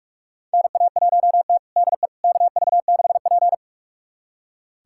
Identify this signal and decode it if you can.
Morse code, heard as a keyed tone, about 46 wpm, 700 Hz. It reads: NA1T DE KV6P
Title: USA Callsign DE Callsign - 4 Characters - 45wpm